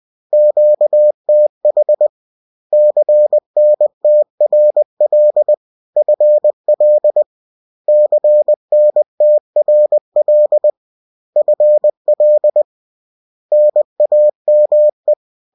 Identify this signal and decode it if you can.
Morse code, heard as a keyed tone, about 20 wpm, 600 Hz. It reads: QTH CNTRL FL CNTRL FL NAME